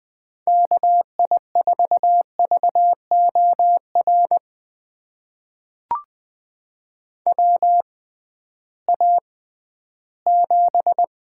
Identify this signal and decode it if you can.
Morse code, heard as a keyed tone, about 20 wpm, spaced as for 13 wpm, 700 Hz. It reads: KI4VOR WA7
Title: USA Call Signs - 6 Characters - 6x Character Spacing - 20wpm